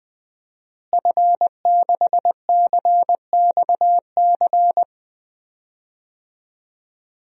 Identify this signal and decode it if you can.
Morse code, heard as a keyed tone, about 20 wpm, 700 Hz. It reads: F6CXC
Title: DXCC-100 Call Signs - 5 Characters - Repeated 2x - 20wpm